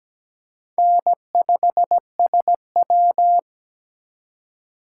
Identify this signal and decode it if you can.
Morse code, heard as a keyed tone, about 20 wpm, 700 Hz. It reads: N5SW
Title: Speed Racing - USA 1x2 Call Signs 17wpm